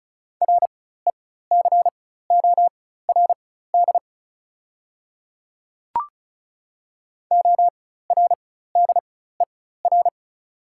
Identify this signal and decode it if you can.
Morse code, heard as a keyed tone, about 34 wpm, spaced as for 16 wpm, 690 Hz. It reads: RECORD ORDER
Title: 4x Character Spacing - Top 500 Words - 35wpm